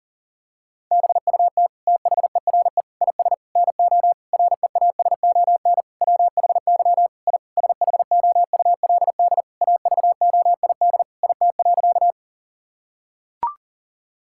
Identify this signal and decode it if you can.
Morse code, heard as a keyed tone, about 40 wpm, 700 Hz. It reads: BUT THERE IS NO REASON WHY I SHOULD AVOID IT.
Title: Sentences from Top 1000 Words - Review Part 6 - 40wpm